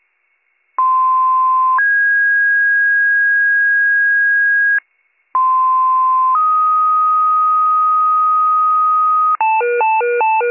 Two-tone page, heard as a one-second tone, then a three-second tone, about 1040 then 1650 Hz, about 1040 then 1250 Hz, with noise. It is heard from 0.8 to 9.4 s.